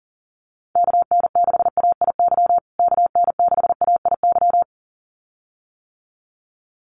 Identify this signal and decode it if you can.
Morse code, heard as a keyed tone, about 40 wpm, 700 Hz. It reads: KN6AIY KN6AIY
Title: USA Call Signs Repeated 2x 40wpm